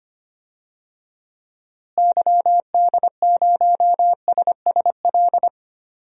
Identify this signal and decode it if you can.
Morse code, heard as a keyed tone, about 25 wpm, 700 Hz. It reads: YD0SSL
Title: DXCC-100 Call Signs - 6 Characters - Repeated 2x - 25wpm